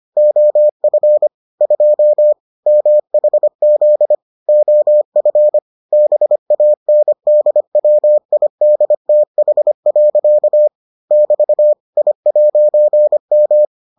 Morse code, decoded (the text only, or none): OF 2 MHZ OF BANDWIDTH. <BT> I'M